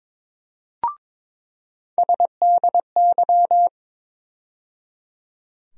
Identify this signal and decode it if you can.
Morse code, heard as a keyed tone, about 22 wpm, 700 Hz. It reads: SDY
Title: USA Call Sign Postfixes - 3 Letters 22wpm